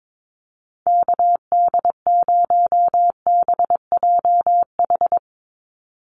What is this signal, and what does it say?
Morse code, heard as a keyed tone, about 22 wpm, 700 Hz. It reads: KD0BJH
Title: USA Call Signs 22wpm